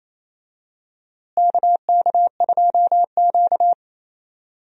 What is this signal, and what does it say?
Morse code, heard as a keyed tone, about 28 wpm, 700 Hz. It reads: KK2Q
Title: USA Call Signs - 4 Characters - Repeated 3x - 28wpm